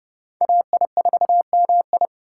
Morse code, heard as a keyed tone, about 30 wpm, 700 Hz. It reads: AI4MI
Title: K1USN SST QSO Practice - 30wpm